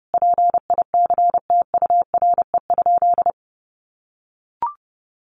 Morse code, heard as a keyed tone, about 30 wpm, 700 Hz. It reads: PICTURE?
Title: A Sound Of Thunder 30wpm